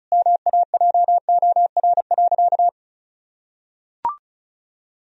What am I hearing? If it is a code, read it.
Morse code, heard as a keyed tone, about 34 wpm, 700 Hz. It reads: MAJOR.
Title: Sentences from Top 1000 Words - Review Part 4 - 35wpm